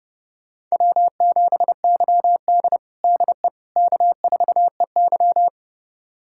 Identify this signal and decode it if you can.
Morse code, heard as a keyed tone, about 30 wpm, 700 Hz. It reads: W7YD DE K4EY
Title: USA Callsign DE Callsign - 4 Characters - 30wpm